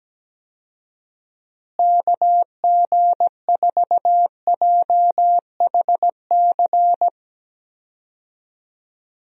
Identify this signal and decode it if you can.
Morse code, heard as a keyed tone, about 17 wpm, 700 Hz. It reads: KG4JHC
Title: USA Call Signs - 6 Characters - Repeated 2x - 17wpm